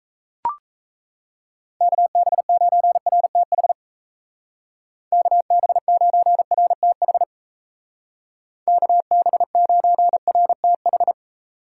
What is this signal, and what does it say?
Morse code, heard as a keyed tone, about 35 wpm, 700 Hz. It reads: KB9RTH KB9RTH KB9RTH
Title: Speed Racing - USA 2x3 Call Signs 28wpm